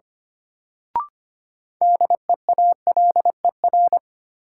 Morse code, heard as a keyed tone, about 25 wpm, 700 Hz. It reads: DEALER
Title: Sentences from Top 1100 Words - New Word Drill - 25wpm